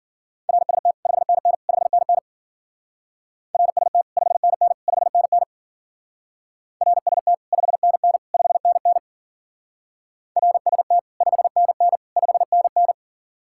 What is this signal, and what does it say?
Morse code, heard as a keyed tone, about 52 wpm, 700 Hz. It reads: RST 5NN 5NN RST 5NN 5NN RST 5NN 5NN RST 5NN 5NN